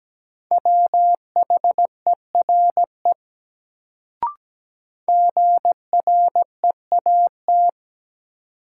Morse code, heard as a keyed tone, about 17 wpm, 700 Hz. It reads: WHERE GREAT
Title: Sentences from Top 200 Words - New Word Drill - 17wpm